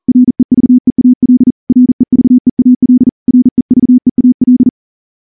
Morse code, heard as a keyed tone, about 38 wpm, 250 Hz. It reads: REVEAL REVEAL REVEAL
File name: morse-training-40WPM